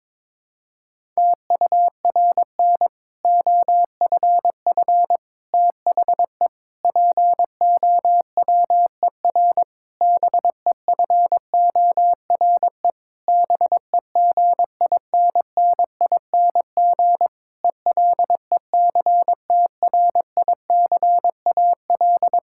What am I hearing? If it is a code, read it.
Morse code, heard as a keyed tone, about 22 wpm, 700 Hz. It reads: TURN OFF THE POWER BEFORE BEGINNING ELECTRICAL
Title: Sentences from Top 1000 Words - Review Part 6 - 22wpm